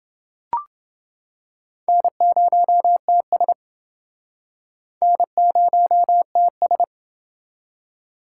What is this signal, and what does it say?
Morse code, heard as a keyed tone, about 29 wpm, 700 Hz. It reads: N0TS N0TS